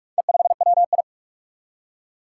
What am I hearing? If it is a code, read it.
Morse code, heard as a keyed tone, about 44 wpm, 700 Hz. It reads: E5WI
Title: Four Challenging Letter-Number Combinations 45wpm